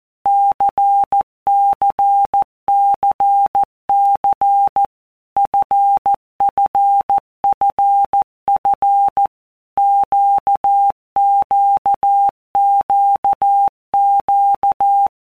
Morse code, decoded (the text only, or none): CCCC FFFF QQQQ